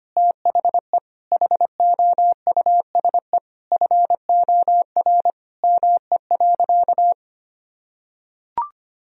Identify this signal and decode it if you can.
Morse code, heard as a keyed tone, about 25 wpm, 700 Hz. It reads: THE HOUSE FOR ME.
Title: Sentences from Top 1000 Words - Review Part 3 - 25wpm